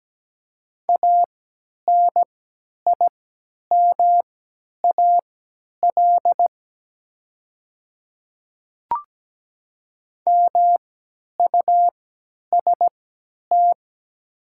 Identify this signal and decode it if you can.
Morse code, heard as a keyed tone, about 17 wpm, spaced as for 10 wpm, 700 Hz. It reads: ANIMAL MUST
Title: 3x Character Spacing - Top 200 Words - 17wpm